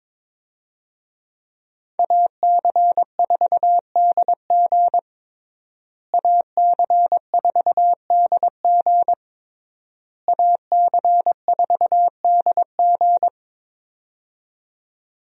Morse code, heard as a keyed tone, about 22 wpm, 700 Hz. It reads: AC4DG AC4DG AC4DG
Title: USA Call Signs - 5 Characters - Repeated 3x - 22wpm